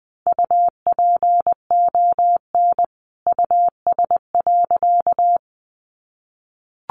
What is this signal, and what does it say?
Morse code, heard as a keyed tone, about 20 wpm, 700 Hz. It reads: UPON US.
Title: Sentences from Top 1500 Words 20wpm